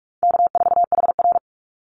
Morse code, heard as a keyed tone, about 46 wpm, 700 Hz. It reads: K4HR
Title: Easy Call Signs - 45wpm